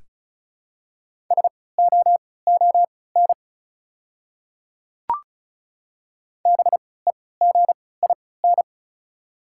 Morse code, heard as a keyed tone, about 36 wpm, spaced as for 20 wpm, 700 Hz. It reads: SOON BEGIN
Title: 3x Character Spacing - Top 500 Words - 35wpm